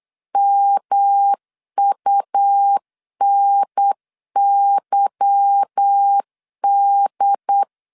Morse code, heard as a keyed tone, about 8 wpm, 800 Hz. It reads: MUNYD
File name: LEZIONI / 10 SIGNS / 10C DOT 8wpm 3649